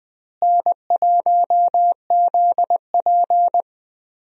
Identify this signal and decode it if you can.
Morse code, heard as a keyed tone, about 20 wpm, 700 Hz. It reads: N1ZP
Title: USA Call Signs - 4 Characters - Repeated 3x - 20wpm